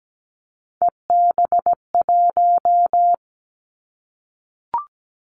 Morse code, heard as a keyed tone, about 17 wpm, 700 Hz. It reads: EB1
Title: Callsign Prefixes for Top 25 DXCC - Repeated 3x - 17wpm